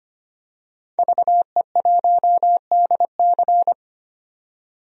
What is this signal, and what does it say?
Morse code, heard as a keyed tone, about 25 wpm, 700 Hz. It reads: VE1DC
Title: DXCC-100 Call Signs - 5 Characters - 25wpm